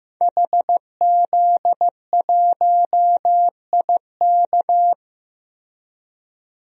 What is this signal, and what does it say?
Morse code, heard as a keyed tone, about 14 wpm, 700 Hz. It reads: HZ1IK
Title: DXCC-100 Call Signs - 5 Characters - Repeated 3x - 15wpm